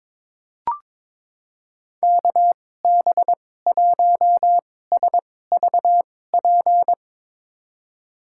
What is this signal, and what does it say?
Morse code, heard as a keyed tone, about 22 wpm, spaced as for 16 wpm, 700 Hz. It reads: KB1SVP
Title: USA Call Signs - 6 Characters - 2x Character Spacing - 22wpm